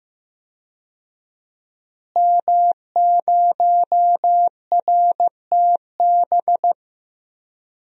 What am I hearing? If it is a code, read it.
Morse code, heard as a keyed tone, about 15 wpm, 700 Hz. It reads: M0RTB
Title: DXCC-100 Call Signs - 5 Characters - Repeated 2x - 15wpm